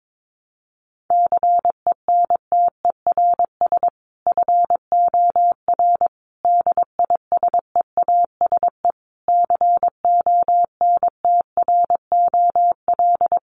Morse code, heard as a keyed tone, about 22 wpm, 700 Hz. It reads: CENTERS FOR DISEASE CONTROL
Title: Sets of 5 Words Encore2 22wpm